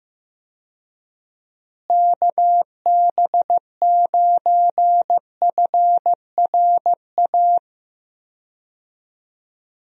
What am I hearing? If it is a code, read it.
Morse code, heard as a keyed tone, about 15 wpm, 700 Hz. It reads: KB9FRA
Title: USA Call Signs - 6 Characters - Repeated 3x - 15wpm